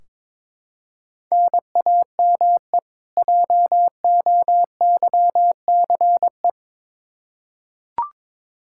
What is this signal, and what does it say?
Morse code, heard as a keyed tone, about 21 wpm, 700 Hz. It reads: NAME JOYCE
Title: K1USN SST QSO Practice 22wpm